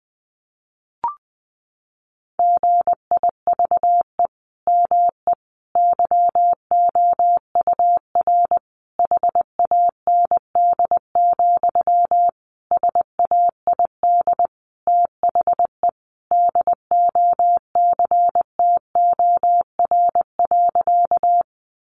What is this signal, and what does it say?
Morse code, heard as a keyed tone, about 20 wpm, 700 Hz. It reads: GIVE ME YOUR HAND, SAID THE DOCTOR.
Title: Sentences from Top 1500 Words 20wpm